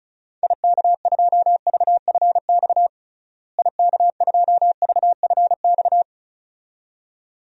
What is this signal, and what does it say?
Morse code, heard as a keyed tone, about 36 wpm, 700 Hz. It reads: IK2VFX IK2VFX
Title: DXCC-100 Call Signs - 6 Characters - Repeated 2x - 35wpm